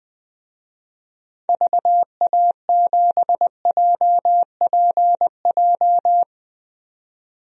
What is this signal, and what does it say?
Morse code, heard as a keyed tone, about 20 wpm, 700 Hz. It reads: VA7JPJ
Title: DXCC-100 Call Signs - 6 Characters - 20wpm